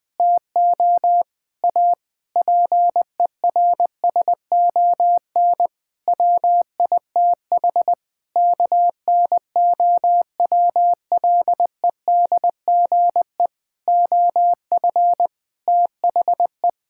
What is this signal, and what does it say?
Morse code, heard as a keyed tone, about 20 wpm, 700 Hz. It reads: TO A PERSON WITH KNOWLEDGE OF THE
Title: Sentences with 9-Word Phrases - Spoken Beginning - 20wpm